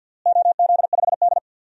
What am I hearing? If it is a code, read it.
Morse code, heard as a keyed tone, about 52 wpm, 700 Hz. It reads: OZ5D
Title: DXCC-100 Call Signs - 4 Characters - Repeated 2x - 50wpm